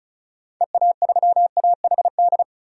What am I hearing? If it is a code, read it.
Morse code, heard as a keyed tone, about 36 wpm, 700 Hz. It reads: EA3AHD
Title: DXCC-100 Call Signs - 6 Characters - 35wpm